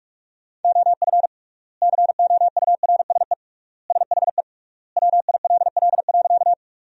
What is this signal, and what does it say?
Morse code, heard as a keyed tone, about 41 wpm, 700 Hz. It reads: OF COURSE SHE WILL.
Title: 3x Word-Spacing - Sentences from Top 500 Words - 45wpm